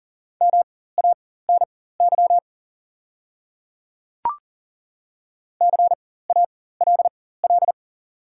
Morse code, heard as a keyed tone, about 37 wpm, spaced as for 18 wpm, 700 Hz. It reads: MANY CALL
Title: 4x Character Spacing - Top 100 Words - 40wpm